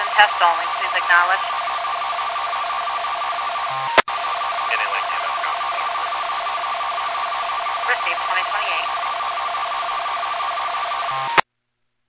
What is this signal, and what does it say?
Two-tone teleprinter signal, about 500 Hz shift, with speech.